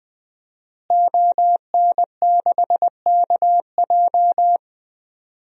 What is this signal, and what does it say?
Morse code, heard as a keyed tone, about 20 wpm, 700 Hz. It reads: ON6KJ
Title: DXCC-100 Call Signs - 5 Characters - Repeated 2x - 20wpm